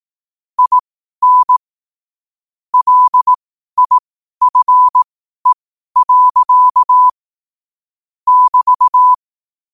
Morse code, heard as a keyed tone, about 18 wpm, spaced as for 12 wpm, 1000 Hz. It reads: IN LIFE. <BT>
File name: qotdfiles / 12wpm